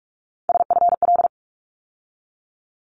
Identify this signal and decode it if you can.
Morse code, heard as a keyed tone, about 46 wpm, 700 Hz. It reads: SFL
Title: ARRL Sections 45wpm